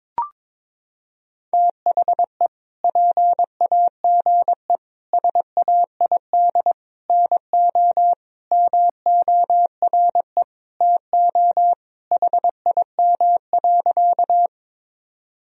Morse code, heard as a keyed tone, about 22 wpm, 700 Hz. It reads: THE PAGE SAID NO MORE TO HIM.
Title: Sentences from Top 1000 Words - Review Part 1 - 22wpm